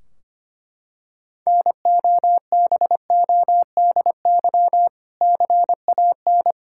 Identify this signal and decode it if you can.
Morse code, heard as a keyed tone, about 25 wpm, 700 Hz. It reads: NOBODY CAN
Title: Sentences from Top 1600 Words 25wpm